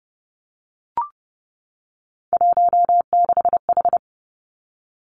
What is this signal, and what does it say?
Morse code, heard as a keyed tone, about 30 wpm, 700 Hz. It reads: 16H
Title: Three Challenging Letter-Number Combinations 30wpm